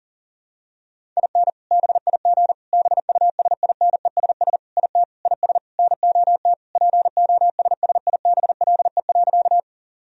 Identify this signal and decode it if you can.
Morse code, heard as a keyed tone, about 40 wpm, 700 Hz. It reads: IN BIG BUSINESS IT IS NOT POSSIBLE.
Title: Sentences from Top 1000 Words - Review Part 1 - 40wpm